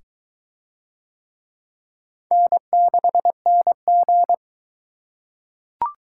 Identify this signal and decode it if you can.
Morse code, heard as a keyed tone, about 21 wpm, 700 Hz. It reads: N6NG